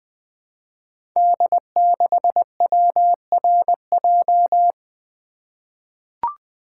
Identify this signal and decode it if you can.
Morse code, heard as a keyed tone, about 20 wpm, 700 Hz. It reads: D6WRJ
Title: Code Groups - 5 Letter:Numbers - 20wpm